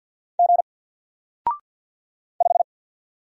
Morse code, heard as a keyed tone, about 48 wpm, 700 Hz. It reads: G 5
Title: Single Letter-Number Alt 50wpm